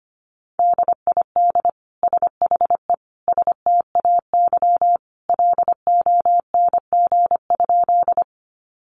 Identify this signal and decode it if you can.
Morse code, heard as a keyed tone, about 25 wpm, 700 Hz. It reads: DID SHE STAY LONG?
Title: Sentences from Top 500 Words 25wpm